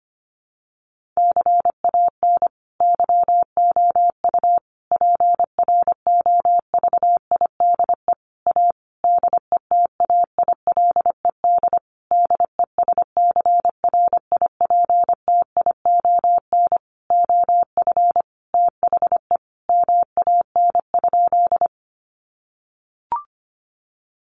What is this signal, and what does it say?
Morse code, heard as a keyed tone, about 25 wpm, 700 Hz. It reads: CAN YOU PROVIDE A DETAILED DESCRIPTION OF THE MAN?
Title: Sentences from Top 1500 Words 25wpm